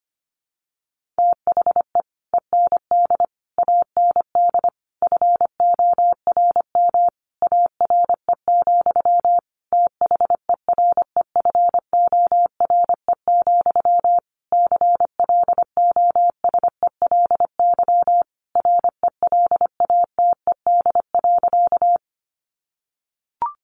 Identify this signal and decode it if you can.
Morse code, heard as a keyed tone, about 25 wpm, 700 Hz. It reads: THE END AND FORM ARE, THEREFORE, CLOSELY RELATED.
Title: Sentences from Top 1400 Words 25wpm